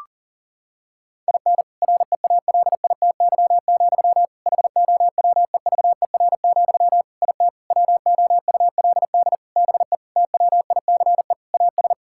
Morse code, heard as a keyed tone, about 40 wpm, 700 Hz. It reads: IN REALITY, HOWEVER, IT WOULD BE TWICE AS
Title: Sentences from Top 1600 Words 40wpm